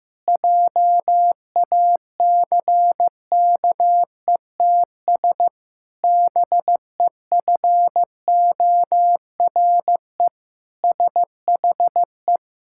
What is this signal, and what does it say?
Morse code, heard as a keyed tone, about 16 wpm, 700 Hz. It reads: JACKETS BEFORE SHE
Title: Pangram Challenge v2 - 15wpm